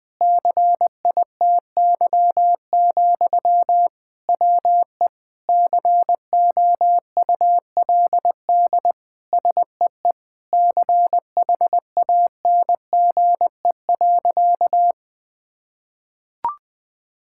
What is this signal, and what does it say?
Morse code, heard as a keyed tone, about 20 wpm, 700 Hz. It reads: CITY, WE COULD SEE CHANGE.
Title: Sentences from Top 1000 Words - Review Part 5 - 20wpm